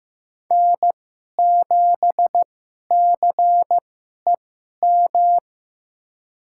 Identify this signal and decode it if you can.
Morse code, heard as a keyed tone, about 15 wpm, spaced as for 11 wpm, 700 Hz. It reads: N7CEM
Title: USA Call Signs - 5 Characters - 2x Character Spacing - 15wpm